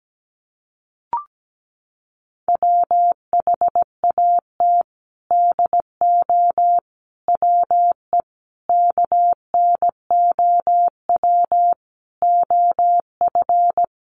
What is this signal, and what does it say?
Morse code, heard as a keyed tone, about 17 wpm, 700 Hz. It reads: WHAT DO WE KNOW OF
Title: Sentences from Top 1400 Words 17wpm